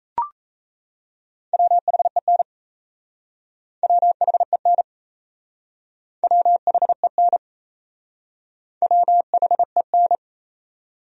Morse code, heard as a keyed tone, about 37 wpm, 700 Hz. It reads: WHEN WHEN WHEN WHEN